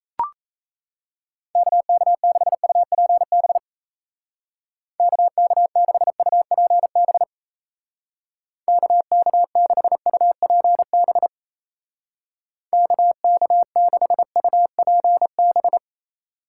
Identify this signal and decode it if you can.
Morse code, heard as a keyed tone, about 37 wpm, 700 Hz. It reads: KK6UPB KK6UPB KK6UPB KK6UPB